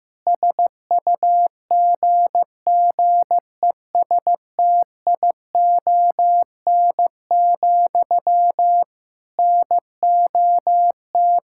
Morse code, heard as a keyed tone, about 15 wpm, 700 Hz. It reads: SUGGESTION, NOT
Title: Sentences from Top 1600 Words 15wpm